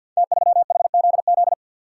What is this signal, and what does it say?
Morse code, heard as a keyed tone, about 51 wpm, 700 Hz. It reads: T3HZZ
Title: Code Groups - 5 Letter:Numbers - 50wpm